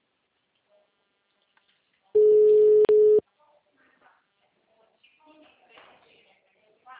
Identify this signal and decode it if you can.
Ringback tone (Europe, 425 Hz), with speech.
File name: out- -2031- - - .29818